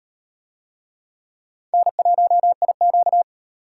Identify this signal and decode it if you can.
Morse code, heard as a keyed tone, about 35 wpm, 700 Hz. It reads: N1IQ